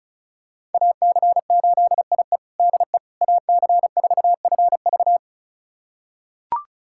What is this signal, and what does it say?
Morse code, heard as a keyed tone, about 34 wpm, 690 Hz. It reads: AC8IE DE AC4FV
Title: USA Callsign DE Callsign - 5 Characters - 35wpm